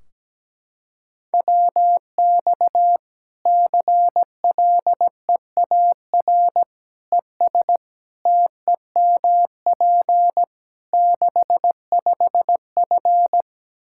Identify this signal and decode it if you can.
Morse code, heard as a keyed tone, about 17 wpm, 700 Hz. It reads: WX CLEAR ES TEMP 65F
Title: Practice QSO - Exchange 2 - Info - 17wpm